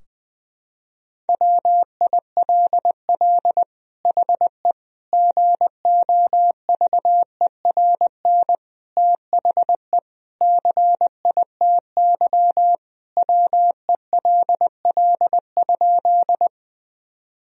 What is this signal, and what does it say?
Morse code, heard as a keyed tone, about 21 wpm, 700 Hz. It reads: WILL HE GOVERN THE CITY WELL?
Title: Sentences from Top 1000 Words - Review Part 2 - 20wpm